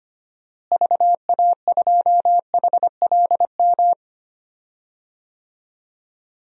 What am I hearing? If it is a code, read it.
Morse code, heard as a keyed tone, about 25 wpm, 700 Hz. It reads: VA2HLM
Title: DXCC-100 Call Signs - 6 Characters - Repeated 3x - 25wpm